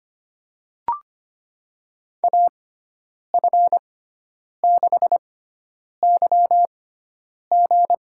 Morse code, heard as a keyed tone, about 25 wpm, spaced as for 9 wpm, 700 Hz. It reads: AF6YG
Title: USA Call Signs - 5 Characters - 6x Character Spacing - 25wpm